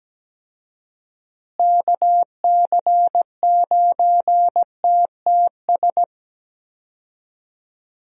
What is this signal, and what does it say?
Morse code, heard as a keyed tone, about 17 wpm, 700 Hz. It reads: KC9TTS
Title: USA Call Signs - 6 Characters - Repeated 3x - 17wpm